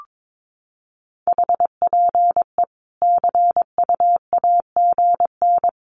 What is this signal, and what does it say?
Morse code, heard as a keyed tone, about 22 wpm, 700 Hz. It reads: HPE CU AGN
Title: Common QSO Phrases 22wpm